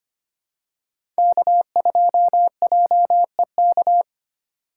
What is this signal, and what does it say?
Morse code, heard as a keyed tone, about 25 wpm, 700 Hz. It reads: K2JEK
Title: USA Call Signs - 5 Characters - 8x Character Spacing - 25wpm